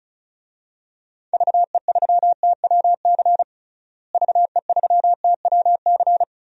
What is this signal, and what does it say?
Morse code, heard as a keyed tone, about 36 wpm, 700 Hz. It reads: VE3TWC VE3TWC
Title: DXCC-100 Call Signs - 6 Characters - Repeated 3x - 35wpm